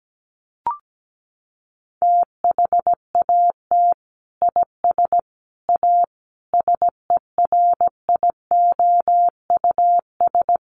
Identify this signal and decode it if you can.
Morse code, heard as a keyed tone, about 17 wpm, 700 Hz. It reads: THAT IS A SERIOUS
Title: Sentences from Top 1100 Words 17wpm